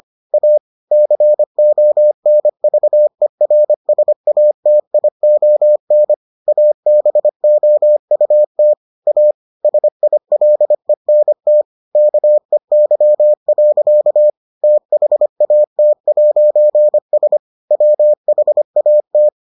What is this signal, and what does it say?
Morse code, heard as a keyed tone, about 25 wpm, 600 Hz. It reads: A CONVERSATION ABOUT A SILENT KEY. THAT'S WHAT